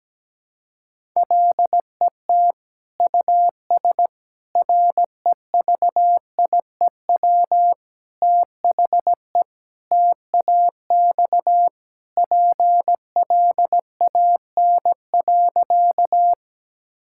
Morse code, heard as a keyed tone, about 17 wpm, 700 Hz. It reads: LET US REVIEW THE TAX PLAN.
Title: Sentences from Top 1000 Words - Review Part 6 - 17wpm